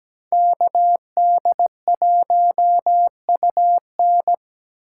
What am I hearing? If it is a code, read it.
Morse code, heard as a keyed tone, about 20 wpm, 700 Hz. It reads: KD1UN
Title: Speed Racing - USA 2x2 Call Signs 17wpm